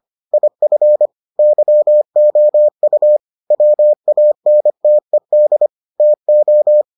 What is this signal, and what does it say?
Morse code, heard as a keyed tone, about 25 wpm, 600 Hz. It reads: IF YOU WANTED TO